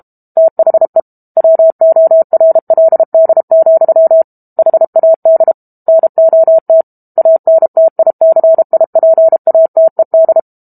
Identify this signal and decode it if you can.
Morse code, heard as a keyed tone, about 32 wpm, 650 Hz. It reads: THE WORLD, HAD NOT ANTICIPATED
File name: Book Two / 06-Chapter 6 - 32 WPM